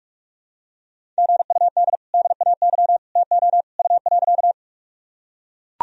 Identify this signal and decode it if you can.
Morse code, heard as a keyed tone, about 42 wpm, 700 Hz. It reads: GUD DAY TO U.
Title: Sentences from Top 200 Words and Abbreviations 45wpm